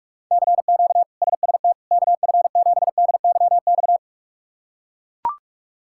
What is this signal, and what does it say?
Morse code, heard as a keyed tone, about 42 wpm, 700 Hz. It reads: CQ SST KF7DYX
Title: K1USN SST QSO Practice - 45wpm